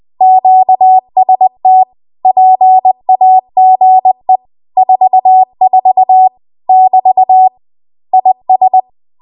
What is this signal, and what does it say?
Morse code, heard as a keyed tone, about 20 wpm, 750 Hz. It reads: QST PAGE 44 <BT> IS
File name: Archive / 20 WPM / 20WPM